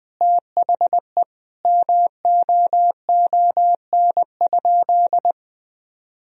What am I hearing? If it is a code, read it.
Morse code, heard as a keyed tone, about 20 wpm, 700 Hz. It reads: THE MOON?
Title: Sentences from Top 1000 Words - Review Part 3 - 20wpm